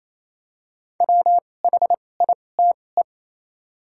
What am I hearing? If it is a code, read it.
Morse code, heard as a keyed tone, about 28 wpm, spaced as for 20 wpm, 700 Hz. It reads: WHITE
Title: 2x Character Spacing - Top 500 Words - 28wpm